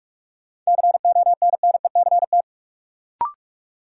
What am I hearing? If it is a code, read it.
Morse code, heard as a keyed tone, about 46 wpm, 700 Hz. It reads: CONNECT
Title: Sentences from Top 1500 Words - New Word Drill Alt - 45wpm